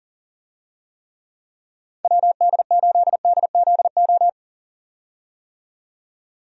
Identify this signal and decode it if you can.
Morse code, heard as a keyed tone, about 40 wpm, 700 Hz. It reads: WD8DZO
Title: USA Call Signs - 6 Characters - 40wpm